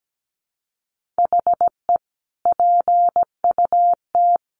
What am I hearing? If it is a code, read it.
Morse code, heard as a keyed tone, about 17 wpm, 700 Hz. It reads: HE PUT
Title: Sentences from Top 1400 Words 17wpm